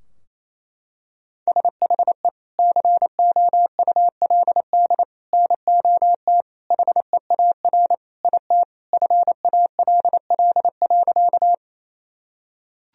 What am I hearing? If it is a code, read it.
Morse code, heard as a keyed tone, about 28 wpm, 700 Hz. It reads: SHE COULD NOT HEAR IT FALL.
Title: Sentences from Top 1000 Words - Review Part 2 - 28wpm